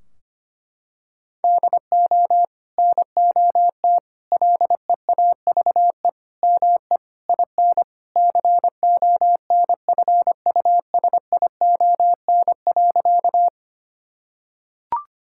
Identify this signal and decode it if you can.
Morse code, heard as a keyed tone, about 25 wpm, 700 Hz. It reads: DO NOT LEAVE ME IN CONFUSION.
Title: Sentences from Top 1600 Words 25wpm